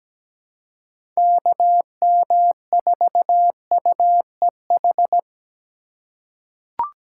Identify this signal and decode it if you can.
Morse code, heard as a keyed tone, about 17 wpm, 700 Hz. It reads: KM4UEH
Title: USA Call Signs - 6 Characters - 4x Character Spacing - 17wpm